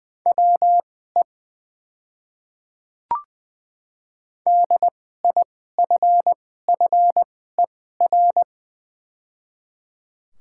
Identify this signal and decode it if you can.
Morse code, heard as a keyed tone, about 20 wpm, spaced as for 14 wpm, 700 Hz. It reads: WE DIFFER
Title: 2x Character Spacing - Top 300 Words - 20wpm